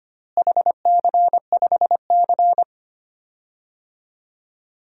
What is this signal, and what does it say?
Morse code, heard as a keyed tone, about 25 wpm, 700 Hz. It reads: HC5C
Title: DXCC-100 Call Signs - 4 Characters - Repeated 3x - 25wpm